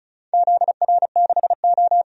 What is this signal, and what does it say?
Morse code, heard as a keyed tone, about 36 wpm, 700 Hz. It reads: ZR6O
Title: DXCC-100 Call Signs - 4 Characters - 35wpm